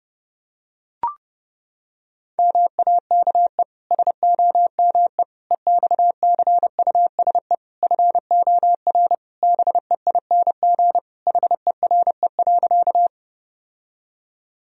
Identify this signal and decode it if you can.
Morse code, heard as a keyed tone, about 30 wpm, 700 Hz. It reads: MAKE SOME EXCUSE FOR BEING HERE.
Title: Sentences from Top 1000 Words - Review Part 3 - 30wpm